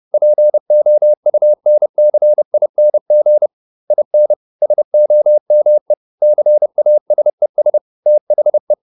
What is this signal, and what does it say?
Morse code, heard as a keyed tone, about 30 wpm, 600 Hz. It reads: POUNCING IN SOME CASES THE